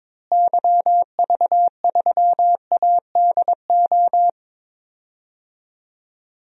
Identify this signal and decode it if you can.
Morse code, heard as a keyed tone, about 22 wpm, 700 Hz. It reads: YV3ADO
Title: DXCC-100 Call Signs - 6 Characters - Repeated 3x - 22wpm